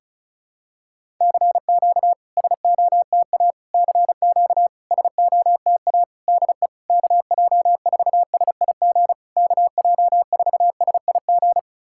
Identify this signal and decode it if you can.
Morse code, heard as a keyed tone, about 36 wpm, 690 Hz. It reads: CQ SOTA CQ SOTA DE KJ4SIG KJ4SIG
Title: SOTA QSO Practice 35wpm